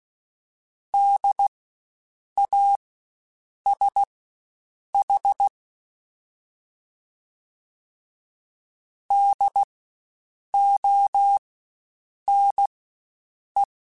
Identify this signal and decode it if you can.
Morse code, heard as a keyed tone, about 16 wpm, spaced as for 7 wpm, 800 Hz. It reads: DASH DONE
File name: Spaced Alphas / Spaced Alphas mp3s / Ds sp @15